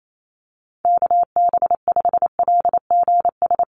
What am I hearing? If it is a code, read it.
Morse code, heard as a keyed tone, about 28 wpm, 700 Hz. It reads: KB5LGS
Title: USA Call Signs Repeated 3x 28wpm